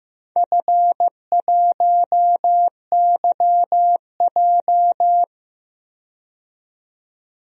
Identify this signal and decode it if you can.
Morse code, heard as a keyed tone, about 15 wpm, 700 Hz. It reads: F1YJ
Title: DXCC-100 Call Signs - 4 Characters - 15wpm